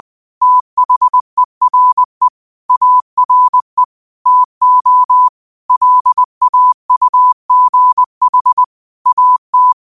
Morse code, heard as a keyed tone, about 20 wpm, 1000 Hz. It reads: THERE ARE TO LAUGH AT